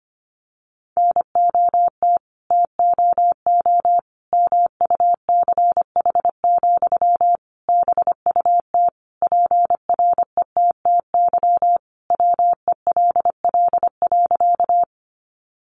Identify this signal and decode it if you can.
Morse code, heard as a keyed tone, about 25 wpm, 700 Hz. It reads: NOT TOO MUCH, BUT PRETTY WELL.
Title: Sentences from Top 1000 Words 25wpm